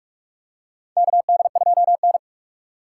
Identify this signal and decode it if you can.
Morse code, heard as a keyed tone, about 46 wpm, 700 Hz. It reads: KD2N
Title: USA Call Signs - 4 Characters - 3x Character Spacing - 45wpm